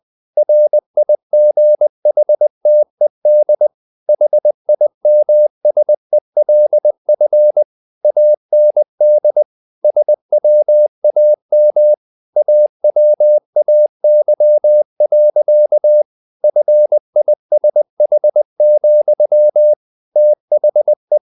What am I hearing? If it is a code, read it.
Morse code, heard as a keyed tone, about 20 wpm, 600 Hz. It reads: RIGHTED HIMSELF AND SWAM AWAY. FISH, THE
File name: the old man and the sea part 2 20wpm